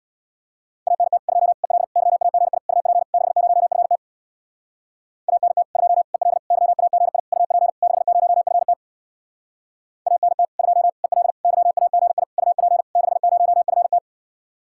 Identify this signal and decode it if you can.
Morse code, heard as a keyed tone, about 67 wpm, 700 Hz. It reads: ANT 2 EL YAGI UP 60FT ANT 2 EL YAGI UP 60FT ANT 2 EL YAGI UP 60FT